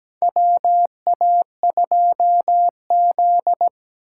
Morse code, heard as a keyed tone, about 17 wpm, 700 Hz. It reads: WA2Z
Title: USA Call Signs - 4 Characters - 2x Character Spacing - 17wpm